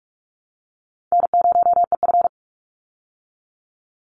Morse code, heard as a keyed tone, about 44 wpm, 700 Hz. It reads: N0EF
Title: Easy Call Signs Repeated 3x - 45wpm